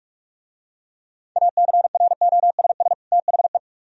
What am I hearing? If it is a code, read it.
Morse code, heard as a keyed tone, about 44 wpm, 700 Hz. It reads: ACROSS THE